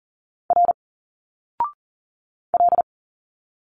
Morse code, heard as a keyed tone, about 39 wpm, 700 Hz. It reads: R L
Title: Single Letters 40wpm